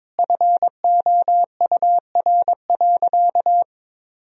Morse code, heard as a keyed tone, about 22 wpm, 700 Hz. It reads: FOUR.
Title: Sentences from Top 1000 Words - Review Part 1 - 22wpm